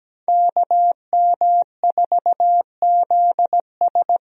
Morse code, heard as a keyed tone, about 17 wpm, 700 Hz. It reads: KM4ZS
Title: USA Call Signs - 5 Characters - Repeated 2x - 17wpm